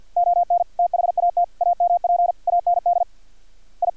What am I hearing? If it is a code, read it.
Morse code, heard as a keyed tone, about 50 wpm, 700 Hz. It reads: ON THAT AMP AND I
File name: archived / CRSnet